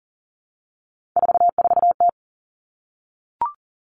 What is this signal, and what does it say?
Morse code, heard as a keyed tone, about 40 wpm, 700 Hz. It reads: 44T
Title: Three Challenging Letter-Number Combinations 40wpm